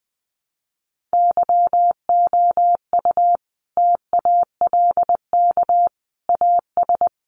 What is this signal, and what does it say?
Morse code, heard as a keyed tone, about 20 wpm, 700 Hz. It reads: YOU TALK AS
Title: Sentences from Top 1100 Words 20wpm